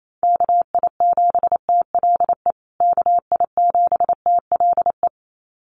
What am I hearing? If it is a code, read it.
Morse code, heard as a keyed tone, about 28 wpm, 700 Hz. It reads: KI7TLE KI7TLE
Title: USA Call Signs Repeated 2x 28wpm